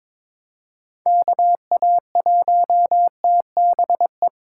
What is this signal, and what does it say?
Morse code, heard as a keyed tone, about 22 wpm, 700 Hz. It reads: KA1TBE
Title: USA Call Signs - 6 Characters - 2x Character Spacing - 22wpm